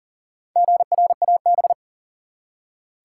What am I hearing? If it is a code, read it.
Morse code, heard as a keyed tone, about 40 wpm, 700 Hz. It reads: GRAB
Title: Four Letter Words - 40wpm